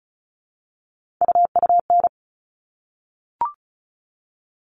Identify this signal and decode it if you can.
Morse code, heard as a keyed tone, about 34 wpm, 700 Hz. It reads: UUN
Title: All Cut Numbers - 3-Digit 35wpm